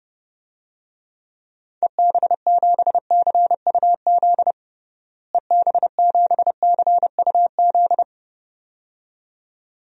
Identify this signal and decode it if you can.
Morse code, heard as a keyed tone, about 30 wpm, 700 Hz. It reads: EB7CUZ EB7CUZ
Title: DXCC-100 Call Signs - 6 Characters - Repeated 2x - 30wpm